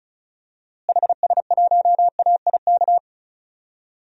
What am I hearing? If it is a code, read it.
Morse code, heard as a keyed tone, about 36 wpm, 700 Hz. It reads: HS1AIK
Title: DXCC-100 Call Signs - 6 Characters - Repeated 2x - 35wpm